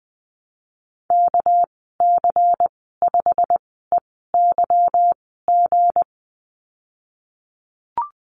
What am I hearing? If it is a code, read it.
Morse code, heard as a keyed tone, about 20 wpm, spaced as for 15 wpm, 700 Hz. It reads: KC5EYG
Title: 207 Lesson 33g - Letter j - Call Signs 20wpm